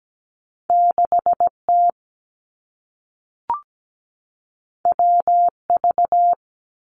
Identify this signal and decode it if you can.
Morse code, heard as a keyed tone, about 17 wpm, 700 Hz. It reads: 6T WV
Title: Two Challenging Letter-Number Combinations 17wpm